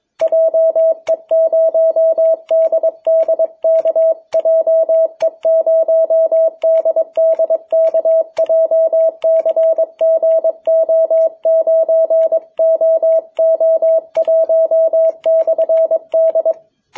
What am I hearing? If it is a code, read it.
Morse code, heard as a keyed tone, about 22 wpm, 660 Hz. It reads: JE0DDK JE0DDK JCGO9OO1/D
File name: jh8cbh / 2021cw